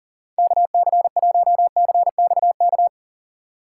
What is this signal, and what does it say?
Morse code, heard as a keyed tone, about 40 wpm, 700 Hz. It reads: KC1CXK
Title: USA Call Signs - 6 Characters - 4x Character Spacing - 40wpm